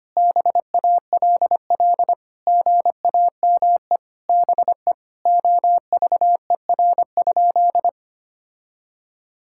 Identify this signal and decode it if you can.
Morse code, heard as a keyed tone, about 25 wpm, 700 Hz. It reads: BALL GAME BE OVER?
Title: Sentences from Top 1000 Words - Review Part 6 - 25wpm